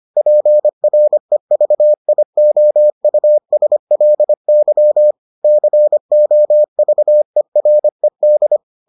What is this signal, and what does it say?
Morse code, heard as a keyed tone, about 25 wpm, 600 Hz. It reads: PREVIOUSLY COVERED